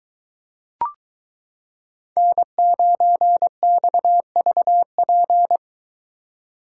Morse code, heard as a keyed tone, about 22 wpm, 700 Hz. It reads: N9XVP